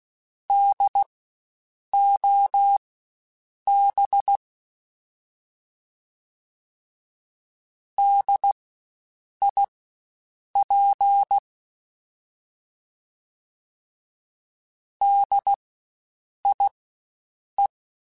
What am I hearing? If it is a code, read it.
Morse code, heard as a keyed tone, about 16 wpm, spaced as for 7 wpm, 800 Hz. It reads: DOB DIP DIE
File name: Spaced Alphas / Spaced Alphas mp3s / Ds sp @15